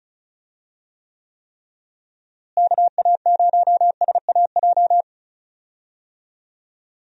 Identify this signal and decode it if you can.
Morse code, heard as a keyed tone, about 36 wpm, 700 Hz. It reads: KA0SAJ
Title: USA Call Signs - 6 Characters - Repeated 3x - 35wpm